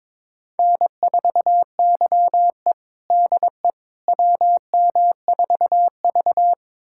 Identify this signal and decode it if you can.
Morse code, heard as a keyed tone, about 22 wpm, 700 Hz. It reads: N4YE DE WM4V
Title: USA Callsign DE Callsign - 4 Characters - 22wpm